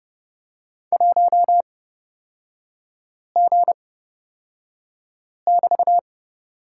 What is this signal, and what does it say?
Morse code, heard as a keyed tone, about 31 wpm, spaced as for 5 wpm, 700 Hz. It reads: 1G<BT>
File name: Lesson 36e - BT - Three Characters 30wpm